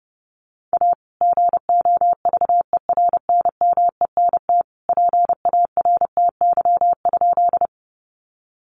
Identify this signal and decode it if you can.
Morse code, heard as a keyed tone, about 30 wpm, 700 Hz. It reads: A GOVERNMENT PARTY?
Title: Sentences from Top 1400 Words 30wpm